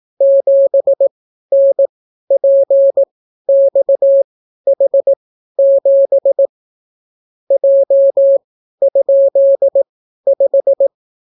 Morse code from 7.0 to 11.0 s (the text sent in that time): J?5